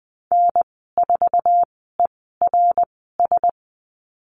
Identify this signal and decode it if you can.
Morse code, heard as a keyed tone, about 20 wpm, spaced as for 15 wpm, 700 Hz. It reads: N4ERS
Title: 088 Lesson 15g - Number 5 - Call Signs 20wpm